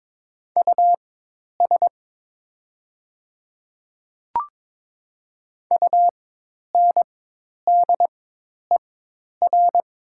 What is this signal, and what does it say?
Morse code, heard as a keyed tone, about 22 wpm, spaced as for 10 wpm, 700 Hz. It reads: US UNDER
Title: 4x Character Spacing - Top 200 Words - 22wpm